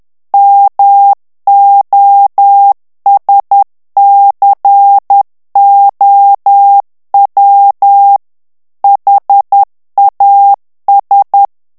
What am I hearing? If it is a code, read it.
Morse code, heard as a keyed tone, about 10 wpm, 800 Hz. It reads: MOSCOW HAS
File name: ANGLAIS texte clair 2 vitesse 10 mots minute